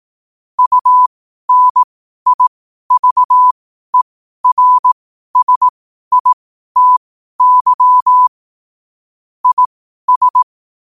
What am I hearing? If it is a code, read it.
Morse code, heard as a keyed tone, about 18 wpm, spaced as for 12 wpm, 1000 Hz. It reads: UNIVERSITY IS
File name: qotdfiles / 12wpm